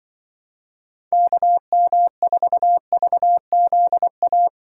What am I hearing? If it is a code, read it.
Morse code, heard as a keyed tone, about 23 wpm, 700 Hz. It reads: KM4VZA
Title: Speed Racing - USA 2x3 Call Signs 20wpm